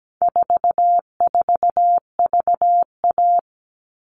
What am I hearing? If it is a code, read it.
Morse code, heard as a keyed tone, about 17 wpm, 700 Hz. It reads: 44VA
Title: All Cut Numbers - 4-Digit 17wpm